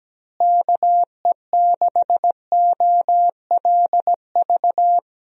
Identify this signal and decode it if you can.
Morse code, heard as a keyed tone, about 17 wpm, 700 Hz. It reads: KE6OLV
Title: USA Call Signs - 6 Characters - Repeated 3x - 17wpm